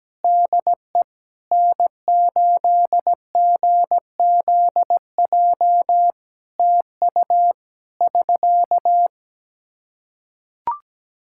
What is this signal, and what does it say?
Morse code, heard as a keyed tone, about 17 wpm, 700 Hz. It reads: DE N8GZJ TU <SK>
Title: Practice QSO - Exchange 4 - Ending - 17wpm